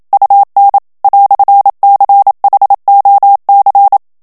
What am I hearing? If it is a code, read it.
Morse code, heard as a keyed tone, about 27 wpm, 800 Hz. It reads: UN "CHOC
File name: FRANCAIS TEXTE CLAIR 1 VITESSE 26 MOTS MINUTES